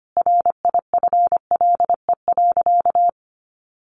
Morse code, heard as a keyed tone, about 25 wpm, 700 Hz. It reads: RIFLE.
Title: A Sound Of Thunder 25wpm